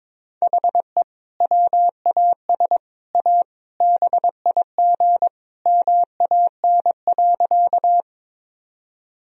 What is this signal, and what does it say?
Morse code, heard as a keyed tone, about 22 wpm, 700 Hz. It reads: HE WAS A BIG MAN.
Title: Sentences from Top 1000 Words - Review Part 1 - 22wpm